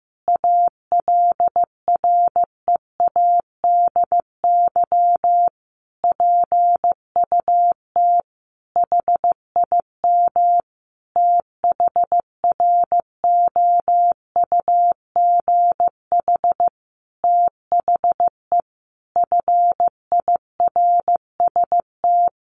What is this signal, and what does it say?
Morse code, heard as a keyed tone, about 15 wpm, 700 Hz. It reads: ALREADY PUT HIM THROUGH THE FIRST
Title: Sentences from Top 1200 Words 15wpm